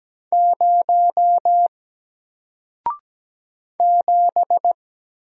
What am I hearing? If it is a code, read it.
Morse code, heard as a keyed tone, about 17 wpm, 700 Hz. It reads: 0 7
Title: Single Letter-Number 17wpm